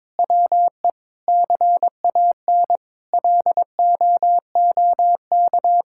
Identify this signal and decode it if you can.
Morse code, heard as a keyed tone, about 21 wpm, 700 Hz. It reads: WE CAN LOOK
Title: Sentences from Top 1000 Words - Review Part 6 - 22wpm